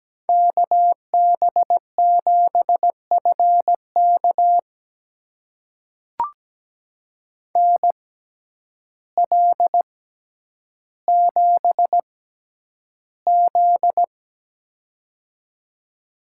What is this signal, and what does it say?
Morse code, heard as a keyed tone, about 17 wpm, spaced as for 9 wpm, 700 Hz. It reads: KB7FK NL7Z
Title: USA Call Signs - 6x Character Spacing - 17wpm